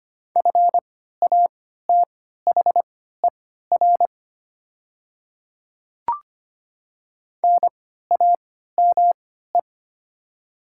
Morse code, heard as a keyed tone, about 25 wpm, spaced as for 14 wpm, 700 Hz. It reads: FATHER NAME
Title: 3x Character Spacing - Top 200 Words - 25wpm